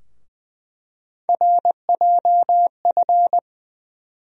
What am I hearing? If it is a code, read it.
Morse code, heard as a keyed tone, about 20 wpm, 700 Hz. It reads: RJF
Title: USA Call Sign Postfixes - 3 Letters 20wpm